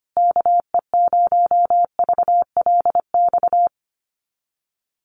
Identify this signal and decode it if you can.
Morse code, heard as a keyed tone, about 25 wpm, 700 Hz. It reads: KE0VLX
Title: USA Call Signs Repeated 3x 25wpm